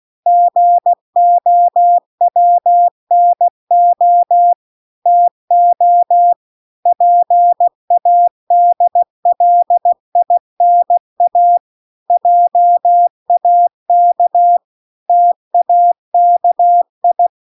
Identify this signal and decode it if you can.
Morse code, heard as a keyed tone, about 16 wpm, 700 Hz. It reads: GOWNO TO PADLINA JAK TAKI